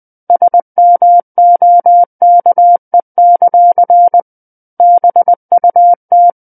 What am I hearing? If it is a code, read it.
Morse code, heard as a keyed tone, about 20 wpm, 700 Hz. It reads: SMOKE; BUT
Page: SKCC Learning Center - Morse Code Practice Files